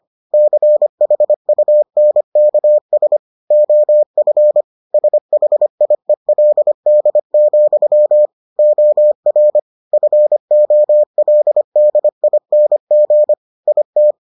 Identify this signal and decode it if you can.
Morse code, heard as a keyed tone, about 25 wpm, 600 Hz. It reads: CHUNKS OF SHIELD, OR FOLDING IT